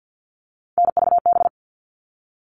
Morse code, heard as a keyed tone, about 50 wpm, 700 Hz. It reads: NVB
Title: All Cut Numbers - 3-Digit 50wpm